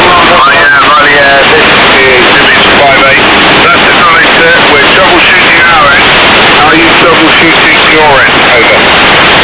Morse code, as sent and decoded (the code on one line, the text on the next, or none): none